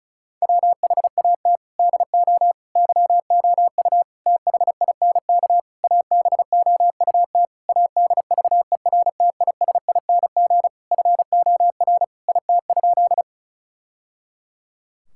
Morse code, heard as a keyed tone, about 36 wpm, 700 Hz. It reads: WHAT DO YOU THINK ABOUT ADVERTISING FOR IT?
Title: Sentences from Top 1000 Words - Review Part 6 - 35wpm